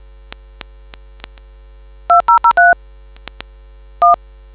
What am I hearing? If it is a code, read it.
DTMF, heard as telephone touch tones, tones of about 130 ms. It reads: 2003 1